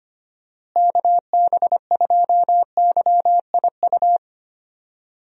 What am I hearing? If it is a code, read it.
Morse code, heard as a keyed tone, about 25 wpm, 700 Hz. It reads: KB2YIU
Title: USA Call Signs - 6 Characters - Repeated 3x - 25wpm